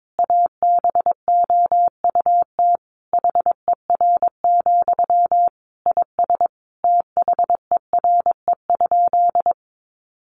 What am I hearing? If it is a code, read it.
Morse code, heard as a keyed tone, about 22 wpm, 700 Hz. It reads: ABOUT HER, IS THERE?
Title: Sentences from Top 1500 Words 22wpm